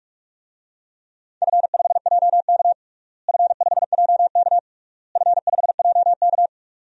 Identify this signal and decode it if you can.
Morse code, heard as a keyed tone, about 46 wpm, 700 Hz. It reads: F5JK F5JK F5JK
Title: DXCC-100 Call Signs - 4 Characters - Repeated 3x - 45wpm